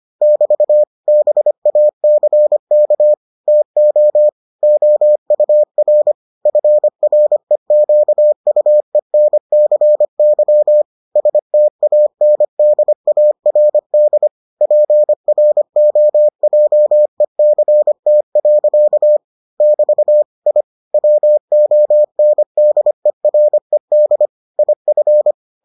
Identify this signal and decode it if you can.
Morse code, heard as a keyed tone, about 25 wpm, 600 Hz. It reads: <BT> BACK TO OUR FREQUENCY STANDARD PROJECT. <BT> I WONDERED IF